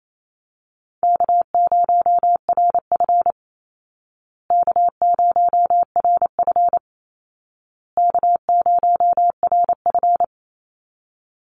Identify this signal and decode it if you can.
Morse code, heard as a keyed tone, about 28 wpm, 700 Hz. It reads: K0RF K0RF K0RF
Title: Easy Call Signs Repeated 3x - 28wpm